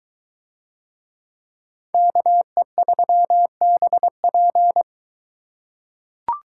Morse code, heard as a keyed tone, about 23 wpm, 700 Hz. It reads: KE3BP